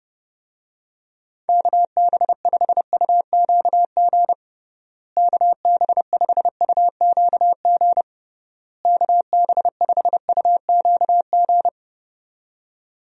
Morse code, heard as a keyed tone, about 30 wpm, 700 Hz. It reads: KB5UQG KB5UQG KB5UQG
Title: USA Call Signs - 6 Characters - Repeated 3x - 30wpm